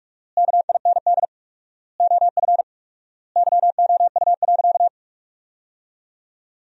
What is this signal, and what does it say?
Morse code, heard as a keyed tone, about 46 wpm, 700 Hz. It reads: KIND OF YOU.
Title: 4x Word-Spacing - Sentences from Top 200 Words - 45wpm